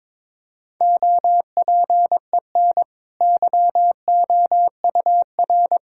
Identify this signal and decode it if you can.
Morse code, heard as a keyed tone, about 22 wpm, 700 Hz. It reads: OPEN YOUR
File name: practice / 22wpm / Easy Sentences with Hints - 22wpm